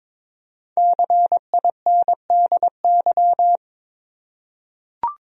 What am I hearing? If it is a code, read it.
Morse code, heard as a keyed tone, about 22 wpm, 700 Hz. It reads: CINDY
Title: USA Common Names - Max 5 Letters - 22wpm